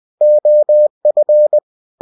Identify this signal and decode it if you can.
Morse code, heard as a keyed tone, about 20 wpm, 600 Hz. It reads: OF
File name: cw-books / winnie-the-pooh / winnie-20wpm-page-0091